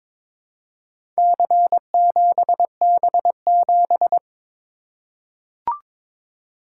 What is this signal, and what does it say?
Morse code, heard as a keyed tone, about 22 wpm, 700 Hz. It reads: C7B7
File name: practice / 22wpm / Four Letter-Number Combinations - 22wpm